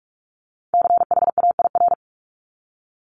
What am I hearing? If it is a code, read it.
Morse code, heard as a keyed tone, about 46 wpm, 700 Hz. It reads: CHAIR
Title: Multi-letter Phonemes 45wpm